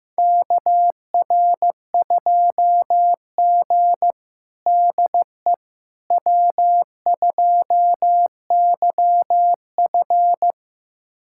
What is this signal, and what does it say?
Morse code, heard as a keyed tone, about 15 wpm, 700 Hz. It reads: KR2G DE W2YF
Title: USA Callsign DE Callsign - 4 Characters - 15wpm